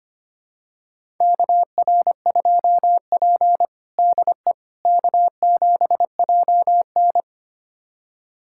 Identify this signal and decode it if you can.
Morse code, heard as a keyed tone, about 25 wpm, 700 Hz. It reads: KR2P DE K7JN
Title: USA Callsign DE Callsign - 4 Characters - 25wpm